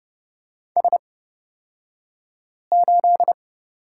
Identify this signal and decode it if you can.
Morse code, heard as a keyed tone, about 29 wpm, spaced as for 5 wpm, 700 Hz. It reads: S8
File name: Lesson 36d - BT - Two Characters 30wpm